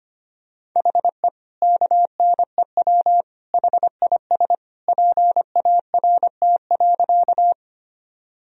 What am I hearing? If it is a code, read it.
Morse code, heard as a keyed tone, about 25 wpm, 700 Hz. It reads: HE KNEW HIS PART.
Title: Sentences from Top 1000 Words - Review Part 1 - 25wpm